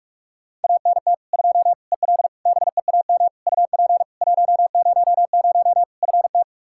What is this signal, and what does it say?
Morse code, heard as a keyed tone, about 43 wpm, 700 Hz. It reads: ANT 2 EL BEAM UP 100 FT
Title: Practice QSO - Exchange 2 - Info - 45wpm